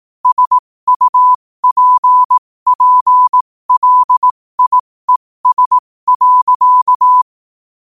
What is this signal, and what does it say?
Morse code, heard as a keyed tone, about 18 wpm, spaced as for 15 wpm, 1000 Hz. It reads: SUPPLIES.
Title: 15 WPM morse code quotes for Fri, 01 Mar 2024